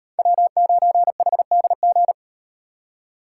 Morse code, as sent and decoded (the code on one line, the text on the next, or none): .-- ----. .... -.. --.
W9HDG